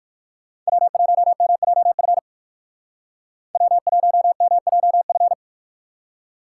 Morse code, heard as a keyed tone, about 62 wpm, 700 Hz. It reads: W1MJF W1MJF